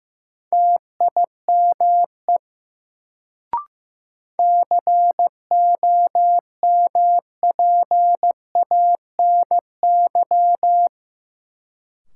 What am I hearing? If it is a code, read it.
Morse code, heard as a keyed tone, about 15 wpm, 700 Hz. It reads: TIME COMPANY
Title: Sets of 2 Words - New Word Drill - 15wpm